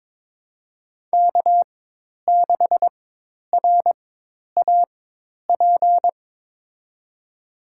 Morse code, heard as a keyed tone, about 22 wpm, spaced as for 10 wpm, 700 Hz. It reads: K6RAP
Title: USA Call Signs - 4x Character Spacing - 22wpm